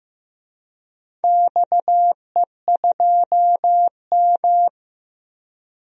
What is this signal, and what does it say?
Morse code, heard as a keyed tone, about 15 wpm, 700 Hz. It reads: XE2M
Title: DXCC-100 Call Signs - 4 Characters - 15wpm